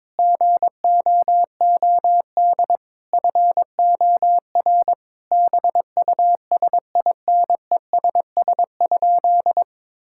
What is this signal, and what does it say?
Morse code, heard as a keyed tone, about 21 wpm, 700 Hz. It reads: GOOD FOR BUSINESS?
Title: Sentences from Top 1000 Words - Review Part 6 - 22wpm